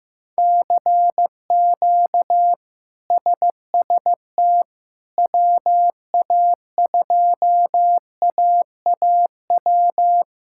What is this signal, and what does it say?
Morse code, heard as a keyed tone, about 15 wpm, 700 Hz. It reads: CQ SST WA2AAW
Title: K1USN SST QSO Practice - 15wpm